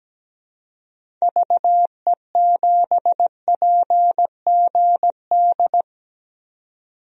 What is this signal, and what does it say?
Morse code, heard as a keyed tone, about 17 wpm, 700 Hz. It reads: VE7PGD
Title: DXCC-100 Call Signs - 6 Characters - 17wpm